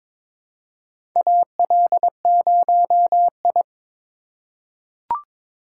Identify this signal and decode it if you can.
Morse code, heard as a keyed tone, about 22 wpm, 700 Hz. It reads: AL0I
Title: USA Call Signs - 4 Characters - 3x Character Spacing - 22wpm